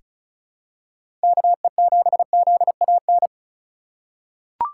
Morse code, heard as a keyed tone, about 36 wpm, 700 Hz. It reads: KE7ZAN
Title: USA Call Signs - 3x Character Spacing - 35wpm